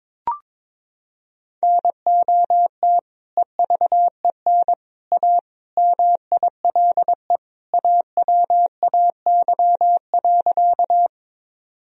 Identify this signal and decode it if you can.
Morse code, heard as a keyed tone, about 22 wpm, 700 Hz. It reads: NOT EVEN A MILE AWAY.
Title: Sentences from Top 1000 Words - Review Part 1 - 22wpm